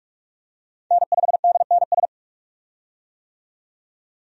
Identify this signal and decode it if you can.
Morse code, heard as a keyed tone, about 43 wpm, 700 Hz. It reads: N5DNS
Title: USA Call Signs - 5 Characters - 45wpm